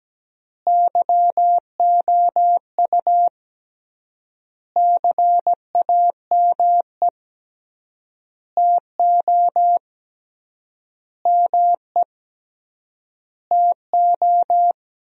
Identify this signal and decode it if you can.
Morse code, heard as a keyed tone, about 17 wpm, 700 Hz. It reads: YOU CAME TO ME TO
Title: 3x Word-Spacing - Sentences from Top 200 Words - 17wpm